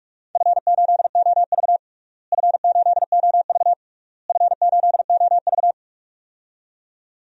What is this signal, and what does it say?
Morse code, heard as a keyed tone, about 41 wpm, 700 Hz. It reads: F8OV F8OV F8OV
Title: DXCC-100 Call Signs - 4 Characters - Repeated 3x - 45wpm